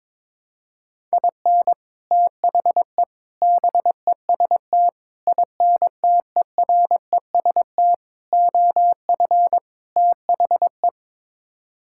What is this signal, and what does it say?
Morse code, heard as a keyed tone, about 22 wpm, 700 Hz. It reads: IN THE BEST INTEREST OF THE
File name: practice / 22wpm / Sets of 6 Words - 22wpm